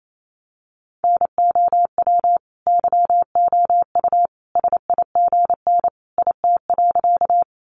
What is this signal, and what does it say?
Morse code, heard as a keyed tone, about 28 wpm, 700 Hz. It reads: NOW YOU SIGN IT.
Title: Sentences from Top 1000 Words 28wpm